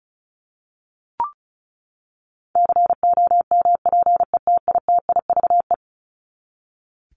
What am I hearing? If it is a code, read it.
Morse code, heard as a keyed tone, about 34 wpm, 700 Hz. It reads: COMPETITIVE
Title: Top 1001-1250 Words 35wpm